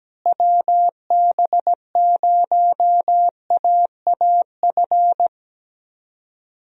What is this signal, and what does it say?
Morse code, heard as a keyed tone, about 17 wpm, 700 Hz. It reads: WB0AAF
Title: USA Call Signs - 6 Characters - Repeated 3x - 17wpm